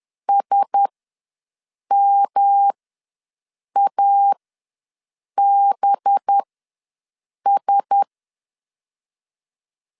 Morse code, heard as a keyed tone, about 10 wpm, spaced as for 6 wpm, 800 Hz. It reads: SMABS
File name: LEZIONI / 5 LBVC / 5 LB ws 10wpm 3616